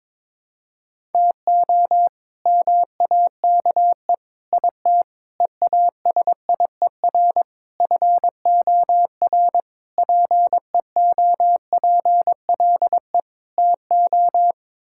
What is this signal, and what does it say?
Morse code, heard as a keyed tone, about 22 wpm, 700 Hz. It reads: TO MAKE IT EASIER FOR PEOPLE TO
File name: practice / 22wpm / Sets of 7 Words - 22wpm